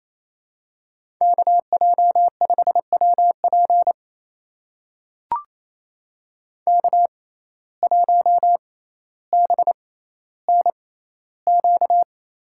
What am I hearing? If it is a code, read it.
Morse code, heard as a keyed tone, about 28 wpm, spaced as for 14 wpm, 700 Hz. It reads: KJ5WP K1BNQ
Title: USA Call Signs - 5 Characters - 6x Character Spacing - 28wpm